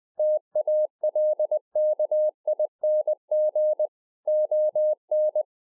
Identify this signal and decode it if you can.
Morse code, heard as a keyed tone, about 20 wpm, 610 Hz. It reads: TALKING ON